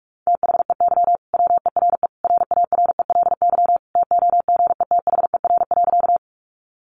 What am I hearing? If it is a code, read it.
Morse code, heard as a keyed tone, about 44 wpm, 700 Hz. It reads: THEY WERE RARELY TOGETHER.
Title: Sentences from Top 1400 Words 45wpm